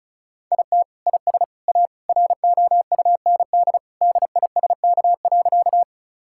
Morse code, heard as a keyed tone, about 36 wpm, 700 Hz. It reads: IT IS A ROUND DISK.
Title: Sentences from Top 1000 Words - Review Part 6 - 35wpm